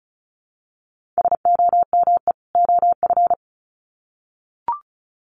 Morse code, heard as a keyed tone, about 34 wpm, 700 Hz. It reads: SOME OF
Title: Sets of 2 Words 35wpm